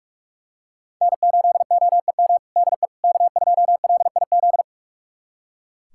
Morse code, heard as a keyed tone, about 41 wpm, 700 Hz. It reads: N8OEM DE K2LIZ
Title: USA Callsign DE Callsign 45wpm